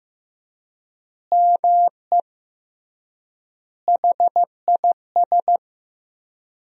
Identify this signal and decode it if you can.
Morse code, heard as a keyed tone, about 15 wpm, 700 Hz. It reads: ME HIS
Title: 3x Word-Spacing - Sentences from Top 200 Words - 15wpm